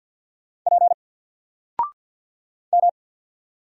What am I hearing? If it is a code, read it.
Morse code, heard as a keyed tone, about 48 wpm, 700 Hz. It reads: P M
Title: Single Character (Letter:Number:Punctuation:Prosign) 50wpm